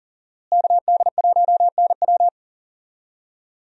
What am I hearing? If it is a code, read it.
Morse code, heard as a keyed tone, about 38 wpm, 700 Hz. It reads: KD1NW
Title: USA Call Signs - 5 Characters - Repeated 3x - 40wpm